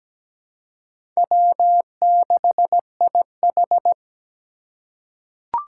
Morse code, heard as a keyed tone, about 17 wpm, 700 Hz. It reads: W6IH
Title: USA Call Signs - 4 Characters - 3x Character Spacing - 17wpm